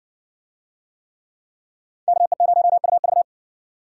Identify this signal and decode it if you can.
Morse code, heard as a keyed tone, about 62 wpm, 700 Hz. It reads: KE0UV